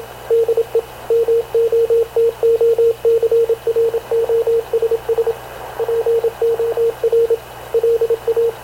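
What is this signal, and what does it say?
Morse code, heard as a keyed tone, about 26 wpm, 460 Hz, with radio noise. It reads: DE MOTOCROSS POR LA
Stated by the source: Havana Radio CLA press (PX) broadcast, recorded by Radio Officer Paul Dumesnil aboard mv CNAV QUEST/CZDO.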